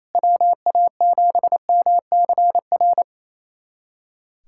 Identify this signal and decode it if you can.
Morse code, heard as a keyed tone, about 28 wpm, 700 Hz. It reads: WA7MCR
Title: USA Callsign DE Callsign - 6 Characters - 28wpm